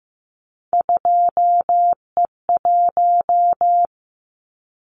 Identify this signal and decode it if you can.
Morse code, heard as a keyed tone, about 15 wpm, 700 Hz. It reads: 2E1
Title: Callsign Prefixes for Top 25 DXCC - 15wpm